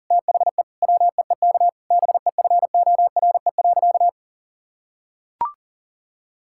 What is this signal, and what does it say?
Morse code, heard as a keyed tone, about 40 wpm, 690 Hz. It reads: THE WEEK BEFORE.
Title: Sentences from Top 1000 Words - Review Part 6 - 40wpm